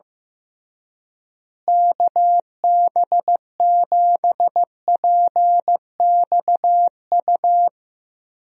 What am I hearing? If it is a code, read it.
Morse code, heard as a keyed tone, about 15 wpm, 700 Hz. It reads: KB7PXU
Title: USA Call Signs - 6 Characters - Repeated 3x - 15wpm